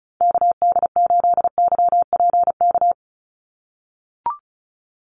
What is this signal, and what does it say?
Morse code, heard as a keyed tone, about 34 wpm, 700 Hz. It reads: KD8YPK
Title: USA Call Signs 35wpm